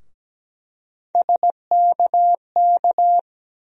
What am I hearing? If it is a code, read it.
Morse code, heard as a keyed tone, about 17 wpm, 700 Hz. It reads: SKK
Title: USA Call Sign Postfixes - 3 Letters 17wpm